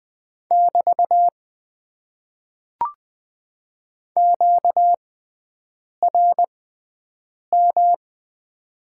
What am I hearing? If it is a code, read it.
Morse code, heard as a keyed tone, about 20 wpm, spaced as for 7 wpm, 700 Hz. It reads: <BT> QRM
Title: Top 90 QSO Elements - 6x Character Spacing - 20wpm